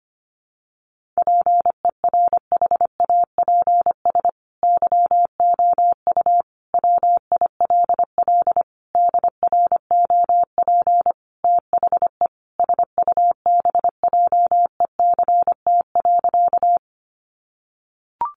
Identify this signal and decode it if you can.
Morse code, heard as a keyed tone, about 25 wpm, 700 Hz. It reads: PERHAPS YOU WILL DROP THE SUBJECT.
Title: Sentences from Top 1400 Words 25wpm